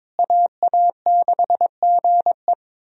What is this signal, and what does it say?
Morse code, heard as a keyed tone, about 21 wpm, 700 Hz. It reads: AA6GE
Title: USA Call Signs - 5 Characters - Repeated 3x - 22wpm